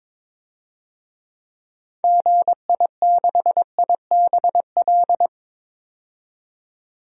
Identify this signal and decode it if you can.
Morse code, heard as a keyed tone, about 22 wpm, 700 Hz. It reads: GI6IBL
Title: DXCC-100 Call Signs - 6 Characters - 22wpm